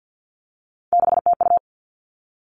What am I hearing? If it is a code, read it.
Morse code, heard as a keyed tone, about 50 wpm, 700 Hz. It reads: 6TU
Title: Three Challenging Letter-Number Combinations 50wpm